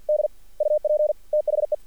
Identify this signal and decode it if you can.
Morse code, heard as a keyed tone, about 48 wpm, 590 Hz. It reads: D UP THE
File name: archived / CRSnet